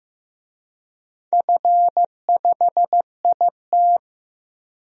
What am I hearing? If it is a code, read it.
Morse code, heard as a keyed tone, about 15 wpm, 700 Hz. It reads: F5IT
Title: DXCC-100 Call Signs - 4 Characters - Repeated 2x - 15wpm